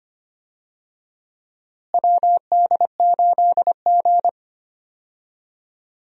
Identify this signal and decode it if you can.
Morse code, heard as a keyed tone, about 25 wpm, 700 Hz. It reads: WD8G
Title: USA Call Signs - 4 Characters - Repeated 2x - 25wpm